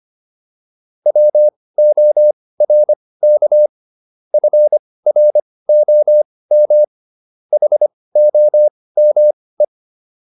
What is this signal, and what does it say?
Morse code, heard as a keyed tone, about 25 wpm, spaced as for 18 wpm, 600 Hz. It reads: WORK FROM HOME
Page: intermediate-practice-files